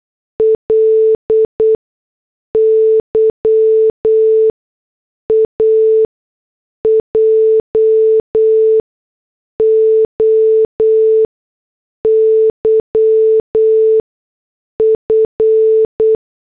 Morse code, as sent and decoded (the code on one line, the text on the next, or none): .-.. / -.-- / .- / .--- / --- / -.-- / ..-.
L Y A J O Y F